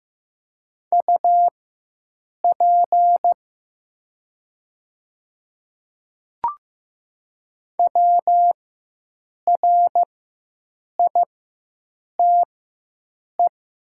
Morse code, heard as a keyed tone, about 15 wpm, spaced as for 7 wpm, 700 Hz. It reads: UP WRITE
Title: 4x Character Spacing - Top 100 Words - 15wpm